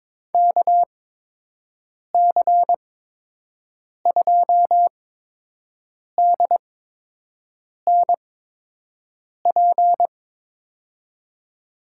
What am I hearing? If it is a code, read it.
Morse code, heard as a keyed tone, about 21 wpm, spaced as for 6 wpm, 700 Hz. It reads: KC2DNP
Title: USA Call Signs - 6 Characters - 8x Character Spacing - 22wpm